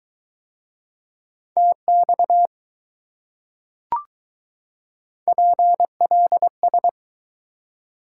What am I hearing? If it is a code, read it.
Morse code, heard as a keyed tone, about 20 wpm, 700 Hz. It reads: TX PLS